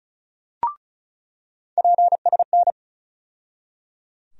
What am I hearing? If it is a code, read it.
Morse code, heard as a keyed tone, about 36 wpm, 700 Hz. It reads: PSN
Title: USA Call Sign Postfixes - 3 Letters 35wpm